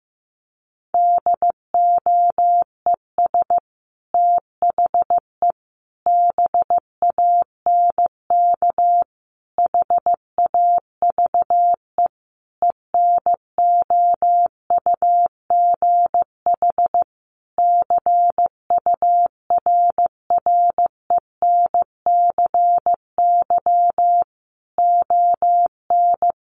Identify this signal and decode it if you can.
Morse code, heard as a keyed tone, about 15 wpm, 700 Hz. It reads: DOES THE BANK HAVE ENOUGH CURRENCY ON
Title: Sentences from Top 1500 Words 15wpm